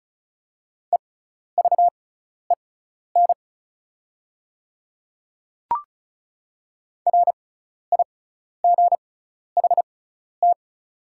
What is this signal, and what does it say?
Morse code, heard as a keyed tone, about 36 wpm, spaced as for 12 wpm, 700 Hz. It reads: EVEN RIGHT
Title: 6x Character Spacing - Top 300 Words - 35wpm